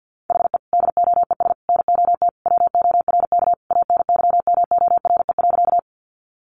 Morse code, heard as a keyed tone, about 50 wpm, 700 Hz. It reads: HE DOES NOT WORK ANYMORE.
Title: Additional Easy Phrases 50wpm